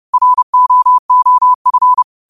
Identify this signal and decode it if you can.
Morse code, heard as a keyed tone, about 30 wpm, 1000 Hz. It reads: ROOF
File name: qotdfiles / 30wpm